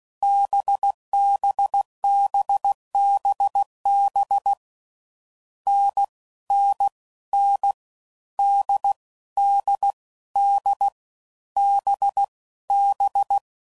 Morse code, heard as a keyed tone, about 16 wpm, 800 Hz. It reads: BBBBB N N N D D D B B
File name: Combinations / Difficult Combos / NDB NDB6